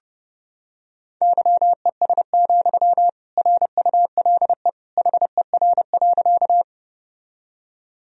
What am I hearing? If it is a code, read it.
Morse code, heard as a keyed tone, about 29 wpm, 700 Hz. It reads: YES, RULE HER.
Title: Sentences from Top 1000 Words - Review Part 2 - 30wpm